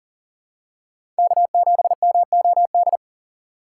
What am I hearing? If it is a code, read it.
Morse code, heard as a keyed tone, about 40 wpm, 700 Hz. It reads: K7MOD
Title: USA Call Signs - 5 Characters - 6x Character Spacing - 40wpm